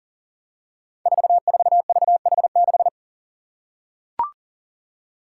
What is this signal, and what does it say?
Morse code, heard as a keyed tone, about 42 wpm, 700 Hz. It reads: 44VH6
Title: Five Challenging Letter-Number Combinations 40wpm